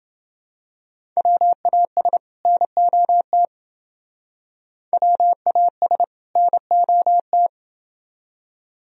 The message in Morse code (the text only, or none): WAS NOT WAS NOT